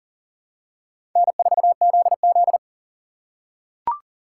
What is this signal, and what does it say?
Morse code, heard as a keyed tone, about 40 wpm, 700 Hz. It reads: N4ZZ
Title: USA Call Signs - 4 Characters - 4x Character Spacing - 40wpm